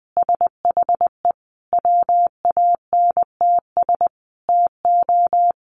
Morse code, heard as a keyed tone, about 20 wpm, 700 Hz. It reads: SHE WANTS TO
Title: Easy Phrases 20wpm